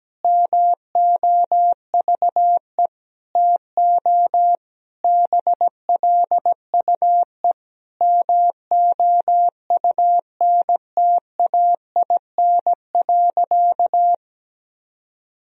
Morse code, heard as a keyed tone, about 17 wpm, 700 Hz. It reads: MOVE TO BLUE MOUNTAIN.
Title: Sentences from Top 1000 Words - Review Part 3 - 17wpm